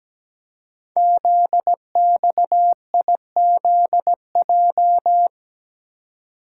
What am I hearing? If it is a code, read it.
Morse code, heard as a keyed tone, about 17 wpm, 700 Hz. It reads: ZXIZJ
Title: Code Groups - 5 Letters - 17wpm